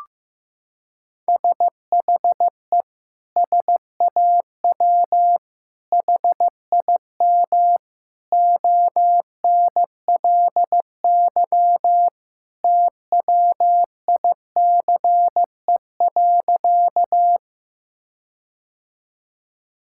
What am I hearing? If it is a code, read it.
Morse code, heard as a keyed tone, about 15 wpm, 700 Hz. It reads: SHE SAW HIM ONLY TWICE.
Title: Sentences from Top 1600 Words 15wpm